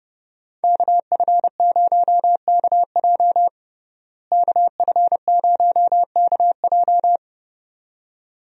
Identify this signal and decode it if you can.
Morse code, heard as a keyed tone, about 30 wpm, 700 Hz. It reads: KF0KJ KF0KJ
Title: USA Call Signs - 5 Characters - Repeated 2x - 30wpm